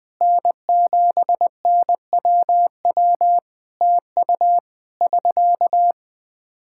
Morse code, heard as a keyed tone, about 20 wpm, 700 Hz. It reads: N7NWW TU <SK>
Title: Practice QSO - Exchange 4 - Ending - 20wpm